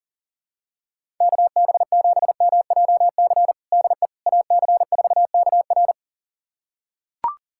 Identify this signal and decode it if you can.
Morse code, heard as a keyed tone, about 40 wpm, 700 Hz. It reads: KB7MJC DE AC4KR
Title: USA Callsign DE Callsign 40wpm